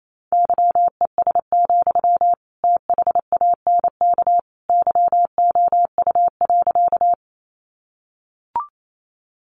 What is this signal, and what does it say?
Morse code, heard as a keyed tone, about 28 wpm, 700 Hz. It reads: YES, THANK YOU.
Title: Sentences from Top 600 Words 28wpm